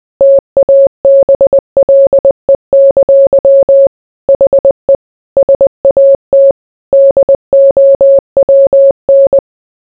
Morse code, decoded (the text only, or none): TABLE! HE SAT DOWN